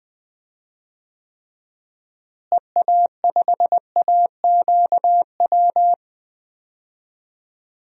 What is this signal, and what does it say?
Morse code, heard as a keyed tone, about 20 wpm, 700 Hz. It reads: EA5AQW
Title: DXCC-100 Call Signs - 6 Characters - Repeated 3x - 20wpm